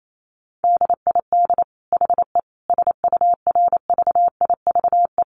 Morse code, heard as a keyed tone, about 28 wpm, 700 Hz. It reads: DID HE SURVIVE
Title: Sentences from Top 1300 Words 28wpm